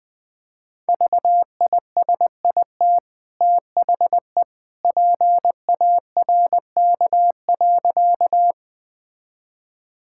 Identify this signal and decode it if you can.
Morse code, heard as a keyed tone, about 20 wpm, 700 Hz. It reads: VISIT THE PARK.
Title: Easy Sentences with Hints - 20wpm